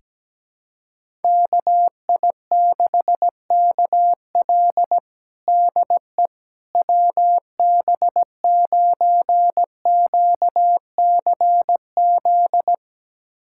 Morse code, heard as a keyed tone, about 17 wpm, 700 Hz. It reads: KI6KL DE WB9QCZ
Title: USA Callsign DE Callsign 17wpm